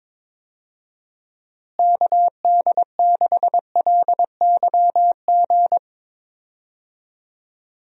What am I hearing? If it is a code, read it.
Morse code, heard as a keyed tone, about 22 wpm, 700 Hz. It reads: KD6LYG
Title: USA Call Signs - 6 Characters - Repeated 2x - 22wpm